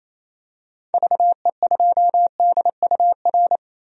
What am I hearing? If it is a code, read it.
Morse code, heard as a keyed tone, about 28 wpm, 700 Hz. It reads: VE2DUR
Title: DXCC-100 Call Signs - 6 Characters - 28wpm